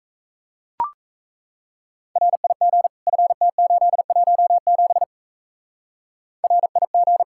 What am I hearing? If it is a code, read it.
Morse code, heard as a keyed tone, about 35 wpm, 700 Hz. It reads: RIG FT817 RIG